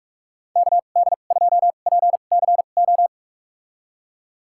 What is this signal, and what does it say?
Morse code, heard as a keyed tone, about 46 wpm, spaced as for 32 wpm, 700 Hz. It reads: KD2PCO
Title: USA Call Signs - 6 Characters - 2x Character Spacing - 45wpm